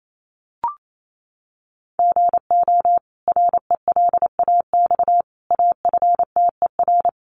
Morse code, heard as a keyed tone, about 28 wpm, 700 Hz. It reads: GO RELAX AFTER
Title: Sentences from Top 1400 Words 28wpm